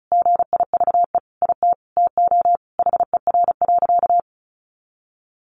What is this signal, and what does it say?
Morse code, heard as a keyed tone, about 34 wpm, 700 Hz. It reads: GIVE IT TO HER.
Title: Sentences from Top 1400 Words 35wpm